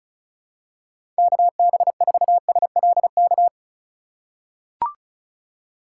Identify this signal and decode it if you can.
Morse code, heard as a keyed tone, about 34 wpm, 700 Hz. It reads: KB4SLK
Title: USA Call Signs - 6 Characters - 2x Character Spacing - 35wpm